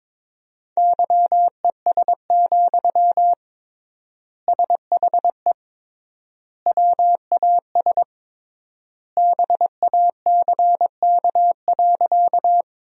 Morse code, heard as a keyed tone, about 21 wpm, 700 Hz. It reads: YES, SHE WAS BACK.
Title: 3x Word-Spacing - Sentences from Top 500 Words - 22wpm